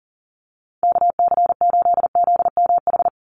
Morse code, heard as a keyed tone, about 40 wpm, 700 Hz. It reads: KC8ZMH
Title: USA Call Signs 40wpm